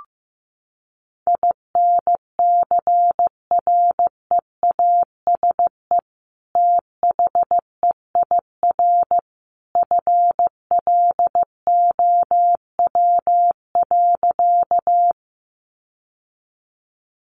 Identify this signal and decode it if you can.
Morse code, heard as a keyed tone, about 15 wpm, 700 Hz. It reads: INCREASE THEIR FLOW.
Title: Sentences from Top 1200 Words 15wpm